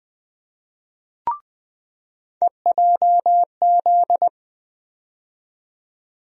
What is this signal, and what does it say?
Morse code, heard as a keyed tone, about 20 wpm, 700 Hz. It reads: EJZ
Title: USA Call Sign Postfixes - 3 Letters 20wpm